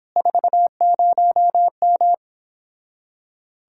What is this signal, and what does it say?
Morse code, heard as a keyed tone, about 31 wpm, 700 Hz. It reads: 40M